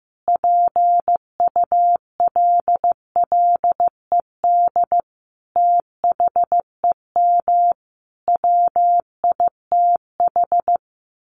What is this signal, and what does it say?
Morse code, heard as a keyed tone, about 15 wpm, 700 Hz. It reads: PULLED THEM WITH
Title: A Sound Of Thunder 15wpm